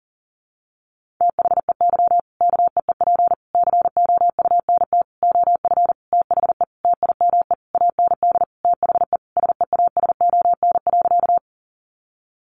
Morse code, heard as a keyed tone, about 40 wpm, 700 Hz. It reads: THEY KEEP COUNT OF THE TIME AND THE SEASON.
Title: Sentences from Top 1500 Words 40wpm